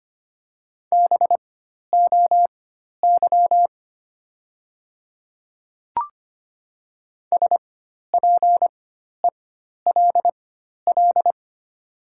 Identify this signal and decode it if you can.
Morse code, heard as a keyed tone, about 25 wpm, spaced as for 12 wpm, 700 Hz. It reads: BOY SPELL
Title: 4x Character Spacing - Top 200 Words - 25wpm